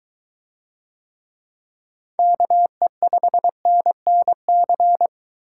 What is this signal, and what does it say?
Morse code, heard as a keyed tone, about 23 wpm, 700 Hz. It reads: KE5NNC